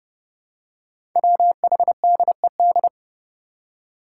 Morse code, heard as a keyed tone, about 30 wpm, 700 Hz. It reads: WHDED
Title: Five Challenging Letter-Number Combinations 30wpm